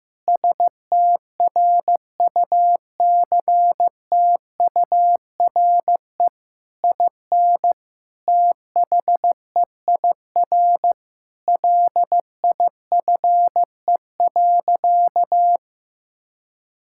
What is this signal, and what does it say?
Morse code, heard as a keyed tone, about 15 wpm, 700 Hz. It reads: STRUCTURE IN THEIR LIFE.
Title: Sentences from Top 1000 Words - Review Part 6 - 15wpm